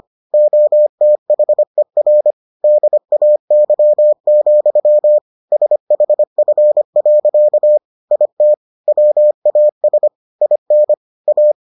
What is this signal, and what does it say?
Morse code, heard as a keyed tone, about 25 wpm, 600 Hz. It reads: OTHER DAY, SHF. IT WAS IN A